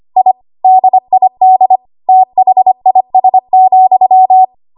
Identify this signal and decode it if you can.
Morse code, heard as a keyed tone, about 25 wpm, 750 Hz. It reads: I DID THIS,
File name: Archive / 25 WPM / 25WPM